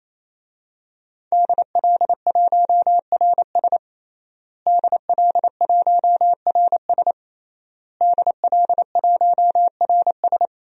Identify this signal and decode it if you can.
Morse code, heard as a keyed tone, about 28 wpm, 700 Hz. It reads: DL1RS DL1RS DL1RS
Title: DXCC-100 Call Signs - 5 Characters - Repeated 3x - 28wpm